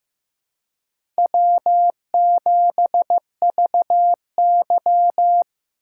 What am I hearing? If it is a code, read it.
Morse code, heard as a keyed tone, about 15 wpm, 700 Hz. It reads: W7VY
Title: USA Call Signs - 4 Characters - 2x Character Spacing - 15wpm